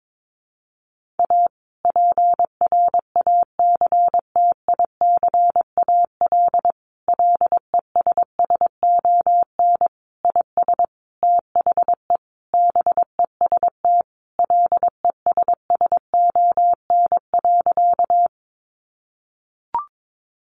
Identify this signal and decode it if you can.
Morse code, heard as a keyed tone, about 22 wpm, 700 Hz. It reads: A PRACTICAL LESSON IS THE BEST LESSON.
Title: Sentences from Top 1400 Words 22wpm